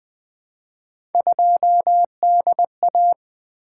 Morse code, heard as a keyed tone, about 20 wpm, 700 Hz. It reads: 2DA
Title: Three Letter-Number Combinations - 20wpm